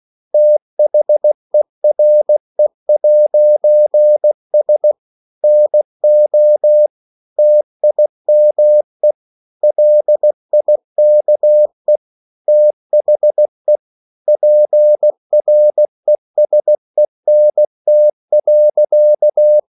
Morse code, heard as a keyed tone, about 16 wpm, 600 Hz. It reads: THERE'S NO TIME LIKE THE PRESENT.
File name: CW/Practice